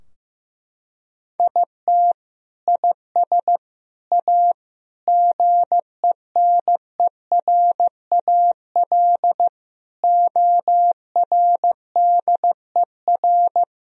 Morse code, heard as a keyed tone, about 15 wpm, 700 Hz. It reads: IT IS A GENERAL ORDER
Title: Sentences from Top 1000 Words - Review Part 3 - 15wpm